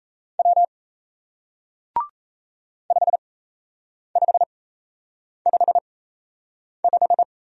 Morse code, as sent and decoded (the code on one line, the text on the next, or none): .-- / ..... ..... ..... .....
W 5555